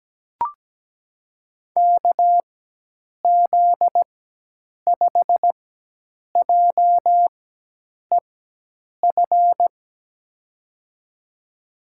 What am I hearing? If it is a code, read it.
Morse code, heard as a keyed tone, about 17 wpm, spaced as for 8 wpm, 700 Hz. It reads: KZ5JEF
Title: USA Call Signs - 6 Characters - 4x Character Spacing - 17wpm